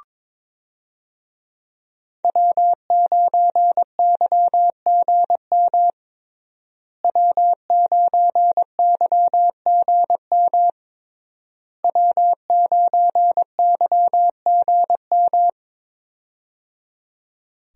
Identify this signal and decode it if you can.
Morse code, heard as a keyed tone, about 22 wpm, 700 Hz. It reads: W9YGM W9YGM W9YGM
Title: USA Call Signs - 5 Characters - Repeated 3x - 22wpm